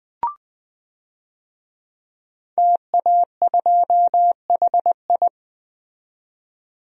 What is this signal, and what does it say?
Morse code, heard as a keyed tone, about 20 wpm, 700 Hz. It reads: TA2HI
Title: DXCC-100 Call Signs - 5 Characters - 20wpm